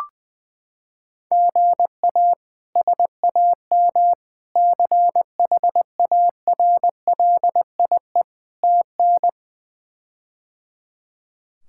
Morse code, heard as a keyed tone, about 20 wpm, 700 Hz. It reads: GA SAM CHARLIE TN
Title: K1USN SST QSO Practice - 20wpm